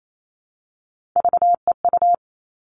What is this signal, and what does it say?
Morse code, heard as a keyed tone, about 28 wpm, 700 Hz. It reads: VEU
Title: All Cut Numbers - 3-Digit 28wpm